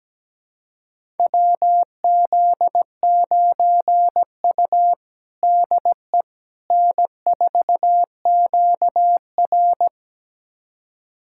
Morse code, heard as a keyed tone, about 17 wpm, 700 Hz. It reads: WZ9U DE N4QR
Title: USA Callsign DE Callsign - 4 Characters - 17wpm